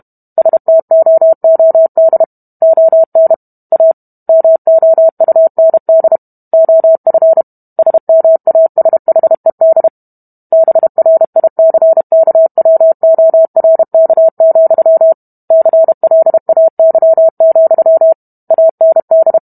Morse code, heard as a keyed tone, about 32 wpm, 650 Hz. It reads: STOOD ON A MOUND OF SMASHED BRICKWORK, CLAY, AND